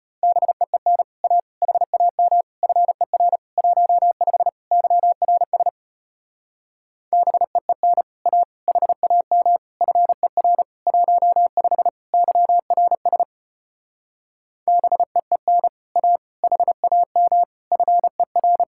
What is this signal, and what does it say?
Morse code, heard as a keyed tone, about 33 wpm, 700 Hz. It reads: BEEN A HAM FER 15 YRS BEEN A HAM FER 15 YRS BEEN A HAM FER